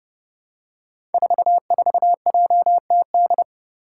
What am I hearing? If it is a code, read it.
Morse code, heard as a keyed tone, about 30 wpm, 700 Hz. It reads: 44JTD
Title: Five Challenging Letter-Number Combinations 30wpm